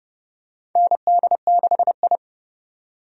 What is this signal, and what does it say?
Morse code, heard as a keyed tone, about 30 wpm, 700 Hz. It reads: ND6I
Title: USA Call Signs - 4 Characters - 30wpm